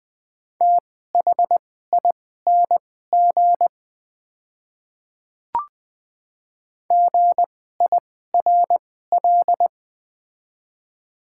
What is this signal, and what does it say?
Morse code, heard as a keyed tone, about 21 wpm, spaced as for 15 wpm, 700 Hz. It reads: THING GIRL
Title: 2x Character Spacing - Top 500 Words - 20wpm